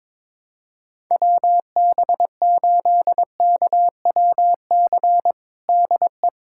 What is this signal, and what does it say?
Morse code, heard as a keyed tone, about 21 wpm, 700 Hz. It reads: WB8KWC DE
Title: USA Callsign DE Callsign - 6 Characters - 22wpm